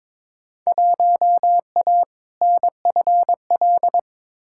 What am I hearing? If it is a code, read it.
Morse code, heard as a keyed tone, about 22 wpm, 700 Hz. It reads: 1A NFL
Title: Field Day Exchanges - 22wpm